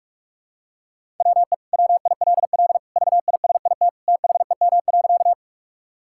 Morse code, heard as a keyed tone, about 46 wpm, 690 Hz. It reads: WE WILL VISIT THEM.
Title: Sentences from Top 1000 Words - Review Part 6 - 45wpm